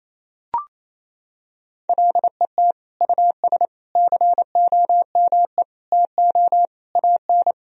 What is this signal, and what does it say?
Morse code, heard as a keyed tone, about 28 wpm, 700 Hz. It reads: LET US COME TO AN
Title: Sentences from Top 1000 Words - Review Part 3 - 28wpm